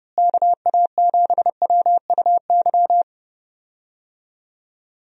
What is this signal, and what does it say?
Morse code, heard as a keyed tone, about 31 wpm, 700 Hz. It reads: KA7WUY
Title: USA Call Signs - 6 Characters - Repeated 2x - 30wpm